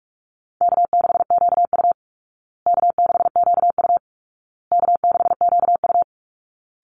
Morse code, heard as a keyed tone, about 44 wpm, 700 Hz. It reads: K6QU K6QU K6QU
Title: Call Signs Repeated 2x - 45wpm